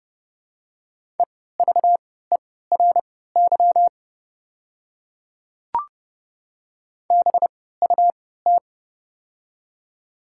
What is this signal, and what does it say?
Morse code, heard as a keyed tone, about 29 wpm, spaced as for 17 wpm, 700 Hz. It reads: EVERY BUT
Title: 3x Character Spacing - Top 300 Words - 30wpm